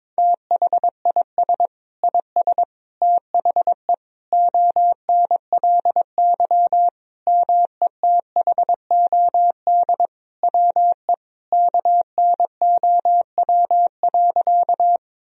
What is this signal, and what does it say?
Morse code, heard as a keyed tone, about 22 wpm, 700 Hz. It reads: THIS IS THE ONLY METHOD WE KNOW.
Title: Sentences from Top 1000 Words - Review Part 3 - 22wpm